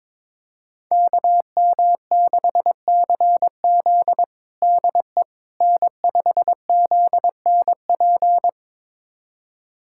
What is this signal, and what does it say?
Morse code, heard as a keyed tone, about 22 wpm, 700 Hz. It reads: KM6CZ DE N5ZNP
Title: USA Callsign DE Callsign - 5 Characters - 22wpm